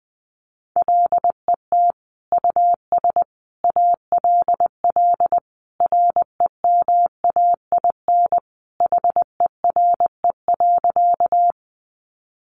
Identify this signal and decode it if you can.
Morse code, heard as a keyed tone, about 20 wpm, 700 Hz. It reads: LET US ALL REMAIN HERE.
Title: Sentences from Top 1200 Words 20wpm